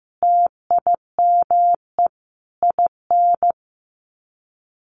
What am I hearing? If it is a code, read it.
Morse code, heard as a keyed tone, about 15 wpm, 700 Hz. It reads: TIME IN
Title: Sets of 5 Words Encore2 15wpm